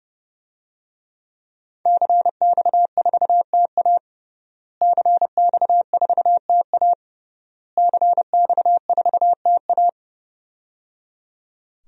Morse code, heard as a keyed tone, about 30 wpm, 700 Hz. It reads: CX4TA CX4TA CX4TA
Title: DXCC-100 Call Signs - 5 Characters - Repeated 3x - 30wpm